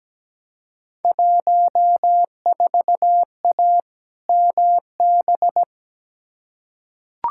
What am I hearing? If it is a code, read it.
Morse code, heard as a keyed tone, about 17 wpm, 700 Hz. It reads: 14A MB
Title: Field Day Exchanges Encore - 17wpm